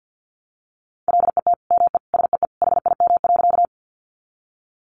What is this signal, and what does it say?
Morse code, heard as a keyed tone, about 50 wpm, 700 Hz. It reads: LET ME SEE HIM.
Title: Sentences from Top 300 Words 50wpm